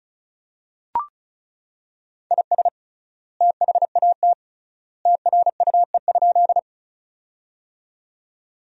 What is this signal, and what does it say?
Morse code, heard as a keyed tone, about 36 wpm, 700 Hz. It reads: IS THAT TRUE?
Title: 3x Word-Spacing - Sentences from Top 400 Words - 35wpm